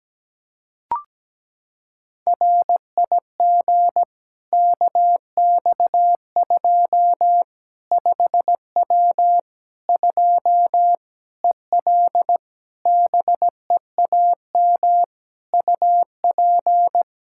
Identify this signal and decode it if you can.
Morse code, heard as a keyed tone, about 17 wpm, 700 Hz. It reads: RIG KX2 5W 2 EL BEAM UP
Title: Practice QSO - Exchange 2 - Info - 17wpm